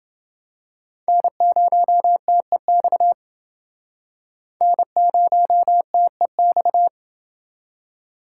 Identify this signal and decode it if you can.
Morse code, heard as a keyed tone, about 29 wpm, 700 Hz. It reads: N0TEX N0TEX